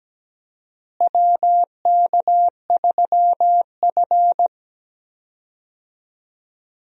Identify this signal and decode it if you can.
Morse code, heard as a keyed tone, about 17 wpm, 700 Hz. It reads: WK3F
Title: USA Call Signs - 4 Characters - Repeated 2x - 17wpm